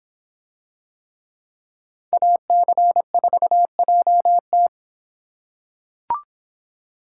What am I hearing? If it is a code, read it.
Morse code, heard as a keyed tone, about 24 wpm, 700 Hz. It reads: AC4JT